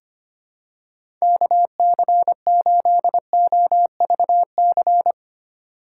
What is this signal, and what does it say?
Morse code, heard as a keyed tone, about 25 wpm, 700 Hz. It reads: KC8OVC
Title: USA Call Signs - 6 Characters - 4x Character Spacing - 25wpm